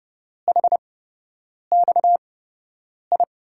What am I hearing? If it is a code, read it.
Morse code, heard as a keyed tone, about 30 wpm, spaced as for 8 wpm, 700 Hz. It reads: HXI
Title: USA Call Signs - 8x Character Spacing - 30wpm